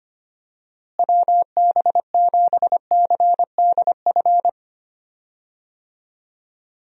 Morse code, heard as a keyed tone, about 25 wpm, 700 Hz. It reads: WB7CDF
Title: USA Call Signs - 6 Characters - Repeated 3x - 25wpm